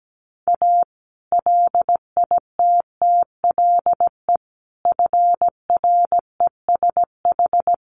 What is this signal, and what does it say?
Morse code, heard as a keyed tone, about 17 wpm, 700 Hz. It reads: A LITTLE FRESH
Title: Sentences from Top 1200 Words 17wpm